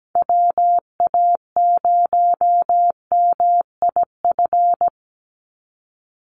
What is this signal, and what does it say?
Morse code, heard as a keyed tone, about 17 wpm, 700 Hz. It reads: WA0MIF
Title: USA Call Signs 17wpm